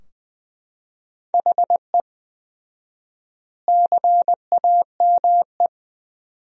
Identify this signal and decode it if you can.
Morse code, heard as a keyed tone, about 20 wpm, 700 Hz. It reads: HE CAME
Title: 4x Word-Spacing - Sentences from Top 300 Words - 20wpm